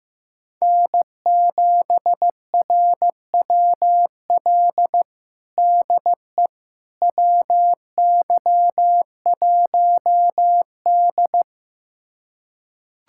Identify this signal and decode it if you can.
Morse code, heard as a keyed tone, about 15 wpm, 700 Hz. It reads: N7RWL DE WY1D
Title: USA Callsign DE Callsign 15wpm